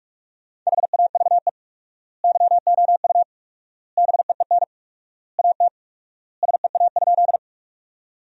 Morse code, heard as a keyed tone, about 46 wpm, 700 Hz. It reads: HAVE YOU BEEN AT SEA?
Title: 4x Word-Spacing - Sentences from Top 300 Words - 45wpm